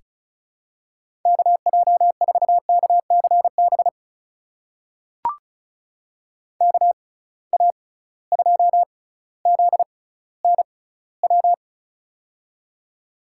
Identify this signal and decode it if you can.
Morse code, heard as a keyed tone, about 36 wpm, spaced as for 18 wpm, 700 Hz. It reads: KJ4KCB KA2ZNW
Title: USA Call Signs - 6 Characters - 6x Character Spacing - 35wpm